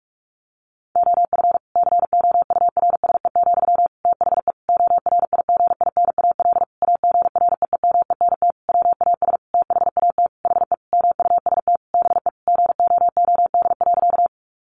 Morse code, heard as a keyed tone, about 44 wpm, 700 Hz. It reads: OF COURSE, THE ORIGINAL AGREEMENT WAS THAT HE MUST BE GOOD.
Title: Sentences from Top 1500 Words 45wpm